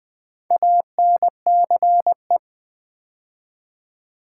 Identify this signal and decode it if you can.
Morse code, heard as a keyed tone, about 20 wpm, 700 Hz. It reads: ANCE
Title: Build A Word - 2 Letters At A Time - 8-Letter Words - 20wpm